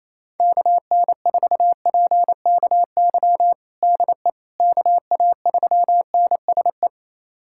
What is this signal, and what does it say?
Morse code, heard as a keyed tone, about 27 wpm, 700 Hz. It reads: KN4PKY DE KA3NSE
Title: USA Callsign DE Callsign - 6 Characters - 28wpm